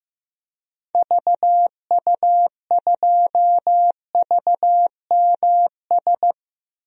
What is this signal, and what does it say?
Morse code, heard as a keyed tone, about 16 wpm, 700 Hz. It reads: VU2VMS
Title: DXCC-100 Call Signs - 6 Characters - Repeated 2x - 15wpm